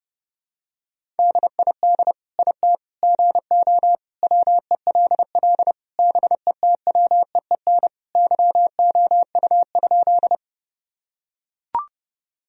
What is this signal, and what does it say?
Morse code, heard as a keyed tone, about 30 wpm, 700 Hz. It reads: DID IT GO WELL BETWEEN YOU?
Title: Sentences from Top 1000 Words - Review Part 1 - 30wpm